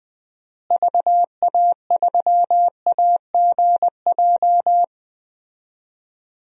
Morse code, heard as a keyed tone, about 19 wpm, 700 Hz. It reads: VA3AGJ
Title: DXCC-100 Call Signs - 6 Characters - Repeated 3x - 20wpm